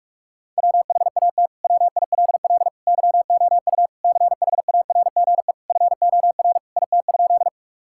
Morse code, heard as a keyed tone, about 46 wpm, 700 Hz. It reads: WHAT WILL YOU CHARGE FOR IT?
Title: Sentences from Top 1000 Words - Review Part 6 - 45wpm